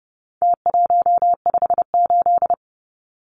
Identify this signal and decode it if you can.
Morse code, heard as a keyed tone, about 30 wpm, 700 Hz. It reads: T158
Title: Numbers - Contest Style - 4-Digit 30wpm